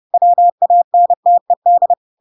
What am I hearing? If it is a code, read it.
Morse code, heard as a keyed tone, about 30 wpm, 700 Hz. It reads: WANTED
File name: Pinocchio / Pinocchio ch8 30 30 ik1hge